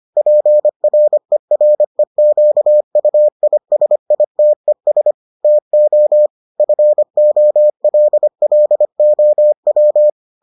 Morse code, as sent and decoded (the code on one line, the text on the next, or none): .--. .-. .-. --.- ..- .. ... .. - . ... / - --- / ..-. --- .-.. .-.. --- .--
PREREQUISITES TO FOLLOW